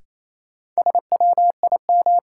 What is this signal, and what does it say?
Morse code, heard as a keyed tone, about 28 wpm, 700 Hz. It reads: SWIM
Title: Sentences from Top 1000 Words - Review Part 6 - New Word Drill - 28wpm